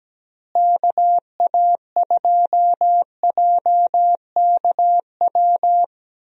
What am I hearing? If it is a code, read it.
Morse code, heard as a keyed tone, about 17 wpm, 700 Hz. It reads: KA2JKW
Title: USA Call Signs - 6 Characters - Repeated 3x - 17wpm